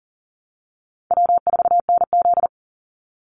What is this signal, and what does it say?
Morse code, heard as a keyed tone, about 41 wpm, 700 Hz. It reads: W4NZ
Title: Easy Call Signs - 40wpm